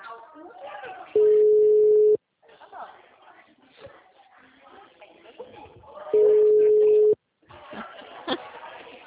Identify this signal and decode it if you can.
Ringback tone (Europe, 425 Hz), with speech.